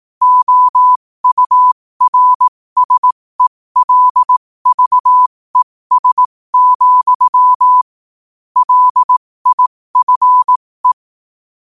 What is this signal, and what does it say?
Morse code, heard as a keyed tone, about 18 wpm, spaced as for 15 wpm, 1000 Hz. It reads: OURSELVES, LIFE
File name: qotdfiles / 15wpm